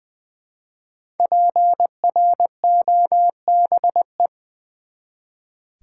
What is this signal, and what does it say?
Morse code, heard as a keyed tone, about 20 wpm, 700 Hz. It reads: PROBE
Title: Build A Word - 1 Letter At A Time - 5-Letter Words - 20wpm